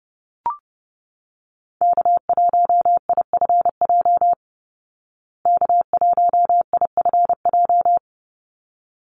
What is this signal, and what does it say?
Morse code, heard as a keyed tone, about 30 wpm, 700 Hz. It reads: K1IFJ K1IFJ
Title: Call Signs Repeated 3x - 30wpm